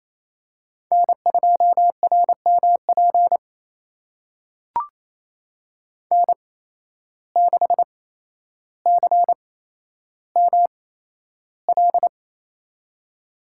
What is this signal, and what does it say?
Morse code, heard as a keyed tone, about 28 wpm, spaced as for 12 wpm, 700 Hz. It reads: N2RMP N6CML
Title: USA Call Signs - 8x Character Spacing - 28wpm